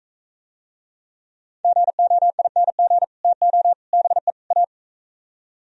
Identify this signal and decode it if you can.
Morse code, heard as a keyed tone, about 45 wpm, 700 Hz. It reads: GOING TO BE A